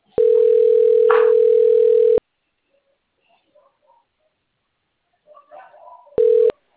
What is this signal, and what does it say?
Ringback tone (North America), 440 + 480 Hz.
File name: out- - - - - .48476